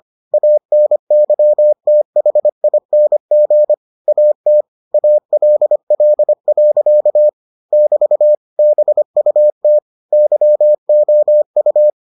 Morse code, heard as a keyed tone, about 25 wpm, 600 Hz. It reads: ANYTHING AT ALL. <BT> BUT YOU